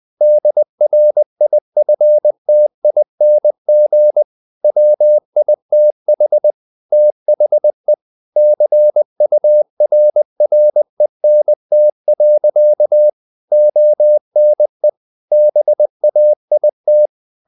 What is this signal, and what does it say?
Morse code, heard as a keyed tone, about 20 wpm, 600 Hz. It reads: DRIFTING WITH THE CURRENT. ONE BAIT
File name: the old man and the sea / the old man and the sea part 1 20wpm